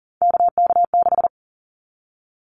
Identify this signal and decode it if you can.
Morse code, heard as a keyed tone, about 40 wpm, 700 Hz. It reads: KK6
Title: Callsign Prefixes for Top 25 DXCC - Repeated 3x - 40wpm